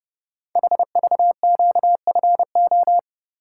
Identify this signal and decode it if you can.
Morse code, heard as a keyed tone, about 30 wpm, 700 Hz. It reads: HVQFO
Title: Code Groups - 5 Letters - 30wpm